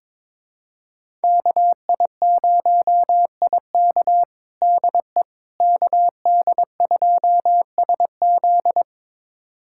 Morse code, heard as a keyed tone, about 22 wpm, 700 Hz. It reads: KI0IK DE KD2SZ
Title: USA Callsign DE Callsign - 5 Characters - 22wpm